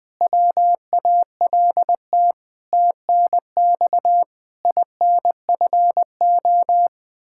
Morse code, heard as a keyed tone, about 20 wpm, 700 Hz. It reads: WALT TNX INFO
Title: Practice QSO - Exchange 3 - More Info - 20wpm